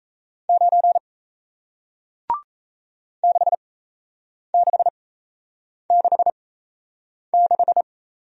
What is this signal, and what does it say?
Morse code, heard as a keyed tone, about 37 wpm, spaced as for 8 wpm, 700 Hz. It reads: 9 6666